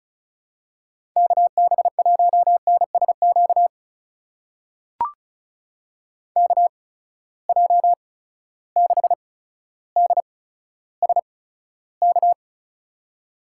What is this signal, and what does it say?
Morse code, heard as a keyed tone, about 36 wpm, spaced as for 15 wpm, 700 Hz. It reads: KB1NSQ KJ6DSK
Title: USA Call Signs - 6 Characters - 8x Character Spacing - 35wpm